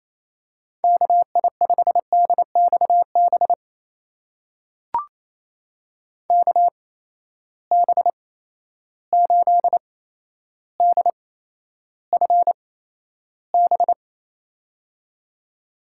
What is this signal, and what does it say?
Morse code, heard as a keyed tone, about 28 wpm, spaced as for 12 wpm, 700 Hz. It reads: KI5DXB KB8DFB
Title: USA Call Signs - 6 Characters - 8x Character Spacing - 28wpm